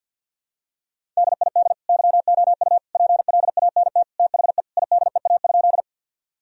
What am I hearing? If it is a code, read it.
Morse code, heard as a keyed tone, about 50 wpm, 700 Hz. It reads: DID YOU PLANT THE IDEA?
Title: Sentences from Top 1000 Words - Review Part 1 - 50wpm